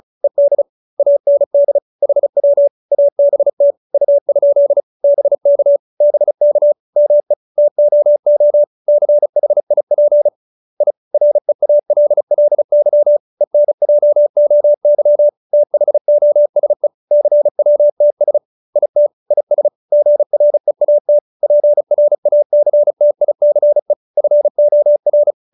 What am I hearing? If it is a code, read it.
Morse code, heard as a keyed tone, about 34 wpm, 590 Hz. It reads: ED AND HW ABT U? BK BK ME TOO CHIP I REALLY ENJOY THOSE CWTS IT IS GREAT PRACTICE FOR